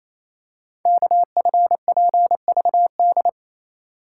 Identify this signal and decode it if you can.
Morse code, heard as a keyed tone, about 29 wpm, 700 Hz. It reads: KFPVD
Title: Code Groups - 5 Letters - 28wpm